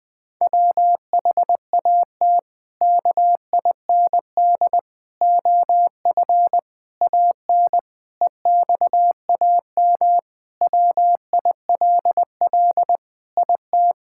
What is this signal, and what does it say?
Morse code, heard as a keyed tone, about 21 wpm, 700 Hz. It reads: WHAT KIND OF AN EXAM WILL IT
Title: Sentences from Top 1000 Words - Review Part 5 - 20wpm